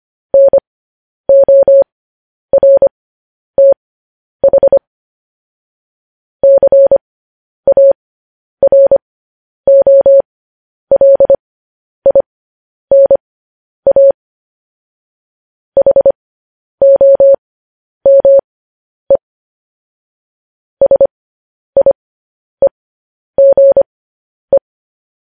Morse code, heard as a keyed tone, about 25 wpm, spaced as for 10 wpm, 550 Hz. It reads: NORTH CAROLINA HOME SIEGE
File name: podcast / News headlines-1x-25x10